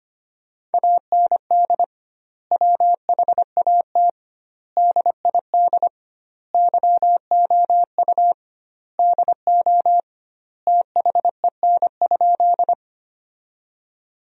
Morse code, heard as a keyed tone, about 25 wpm, 700 Hz. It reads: AND WHAT DID YOU DO THEN?
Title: 2x Word-Spacing - Sentences from Top 100 Words - 25wpm